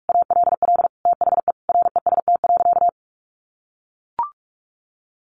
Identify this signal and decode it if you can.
Morse code, heard as a keyed tone, about 44 wpm, 700 Hz. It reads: ALL THE REST.
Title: Sentences from Top 1300 Words 45wpm